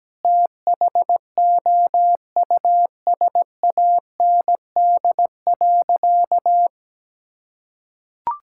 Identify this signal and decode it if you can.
Morse code, heard as a keyed tone, about 17 wpm, 700 Hz. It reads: THOUSAND.
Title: Sentences from Top 1000 Words - Review Part 5 - 17wpm